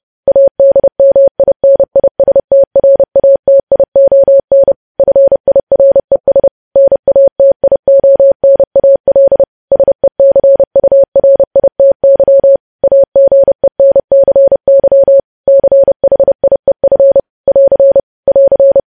Morse code, heard as a keyed tone, about 30 wpm, 550 Hz. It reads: ADMINISTRATION FIRES NATIONAL SECURITY AGENCY CHIEF <AR> <AR>
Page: News Headlines in Morse Code at 30 WPM